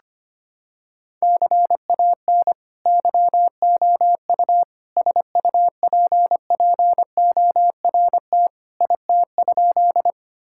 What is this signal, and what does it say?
Morse code, heard as a keyed tone, about 25 wpm, 700 Hz. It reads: CAN YOU SUPPORT IT?
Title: Sentences from Top 1000 Words - Review Part 5 - 25wpm